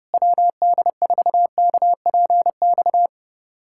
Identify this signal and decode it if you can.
Morse code, heard as a keyed tone, about 30 wpm, 700 Hz. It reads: WD4KPX
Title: USA Callsign DE Callsign 30wpm